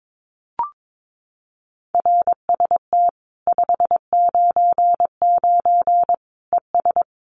Morse code, heard as a keyed tone, about 22 wpm, 700 Hz. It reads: RST 599 ES
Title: Common QSO Phrases 22wpm